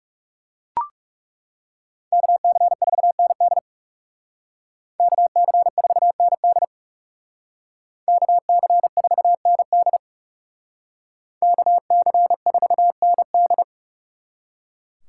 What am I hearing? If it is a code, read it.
Morse code, heard as a keyed tone, about 40 wpm, 700 Hz. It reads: KC4ND KC4ND KC4ND KC4ND